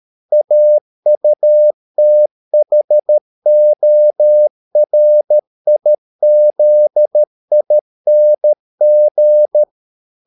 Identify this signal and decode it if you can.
Morse code, heard as a keyed tone, about 13 wpm, 600 Hz. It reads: AUTHORIZING